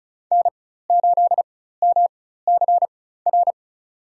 Morse code, heard as a keyed tone, about 34 wpm, spaced as for 16 wpm, 700 Hz. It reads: N8MCR
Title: USA Call Signs - 5 Characters - 4x Character Spacing - 35wpm